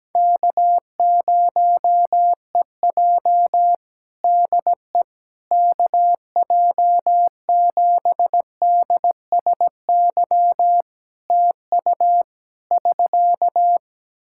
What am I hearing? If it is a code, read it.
Morse code, heard as a keyed tone, about 17 wpm, 700 Hz. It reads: K0EJ DE KJ7DSY TU <SK>
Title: Practice QSO - Exchange 4 - Ending - 17wpm